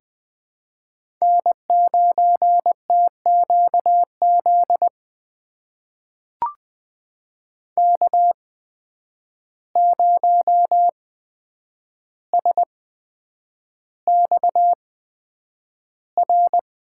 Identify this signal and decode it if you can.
Morse code, heard as a keyed tone, about 20 wpm, spaced as for 9 wpm, 700 Hz. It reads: N9TQZ K0SXR
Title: USA Call Signs - 5 Characters - 8x Character Spacing - 20wpm